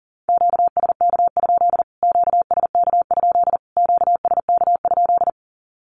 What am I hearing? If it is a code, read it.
Morse code, heard as a keyed tone, about 40 wpm, 700 Hz. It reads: QSK? QSK? QSK?
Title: Q-Codes Repeated 3x 40wpm